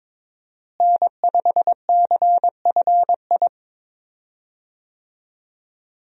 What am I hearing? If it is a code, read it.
Morse code, heard as a keyed tone, about 22 wpm, 700 Hz. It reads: N5CFI
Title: USA Call Signs - 5 Characters - Repeated 3x - 22wpm